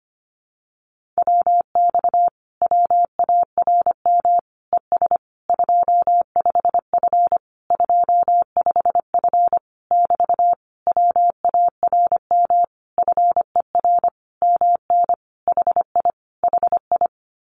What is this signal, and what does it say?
Morse code, heard as a keyed tone, about 25 wpm, 700 Hz. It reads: WX WARM ES 25F 25F <BT> WARM FER MN HI HI
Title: Common QSO Phrases Encore 25wpm